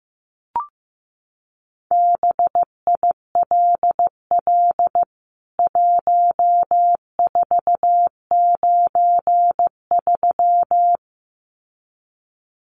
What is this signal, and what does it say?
Morse code, heard as a keyed tone, about 15 wpm, 700 Hz. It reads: BILL 1493
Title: CWT Exchange 15wpm